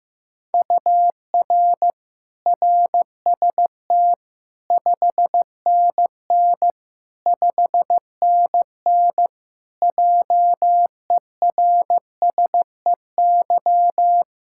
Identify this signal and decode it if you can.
Morse code, heard as a keyed tone, about 16 wpm, 700 Hz. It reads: UR RST 5NN 5NN JERSEY
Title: Practice QSO - Exchange 1 - Beginning - 15wpm